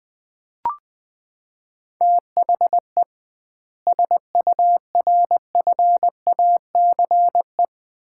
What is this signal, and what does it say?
Morse code, heard as a keyed tone, about 20 wpm, 700 Hz. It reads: THE SURFACE
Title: 2x Word-Spacing - Sets of 4 Words - 20wpm